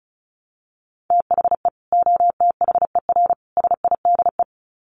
Morse code, heard as a keyed tone, about 34 wpm, 700 Hz. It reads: THE OTHER SIDE
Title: Sets of 3 Words 35wpm